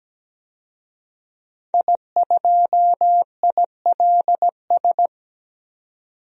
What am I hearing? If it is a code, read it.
Morse code, heard as a keyed tone, about 17 wpm, 700 Hz. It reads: I2ILS
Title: DXCC-100 Call Signs - 5 Characters - 17wpm